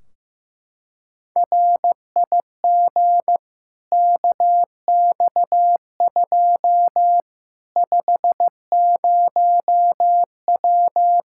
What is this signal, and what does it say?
Morse code, heard as a keyed tone, about 15 wpm, 700 Hz. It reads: RIG KX2 50W
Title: Practice QSO - Exchange 2 - Info - 15wpm